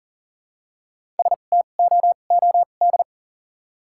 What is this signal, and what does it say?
Morse code, heard as a keyed tone, about 40 wpm, 700 Hz. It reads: S T O O D
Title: 2x Character Spacing - Top 500 Words - 40wpm